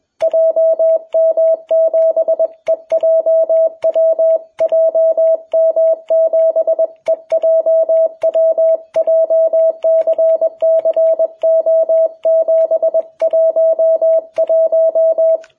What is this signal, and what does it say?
Morse code, heard as a keyed tone, about 21 wpm, 660 Hz. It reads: JM7EJW JM7EJW JCCO711
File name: jh8cbh / 2021cw